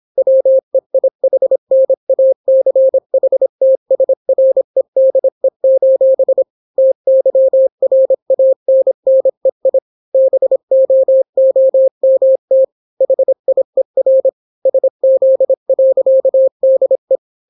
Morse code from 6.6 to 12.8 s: TYRANNEI BOOMT